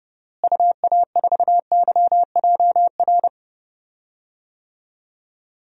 Morse code, heard as a keyed tone, about 30 wpm, 700 Hz. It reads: UA4YJR
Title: DXCC-100 Call Signs - 6 Characters - Repeated 3x - 30wpm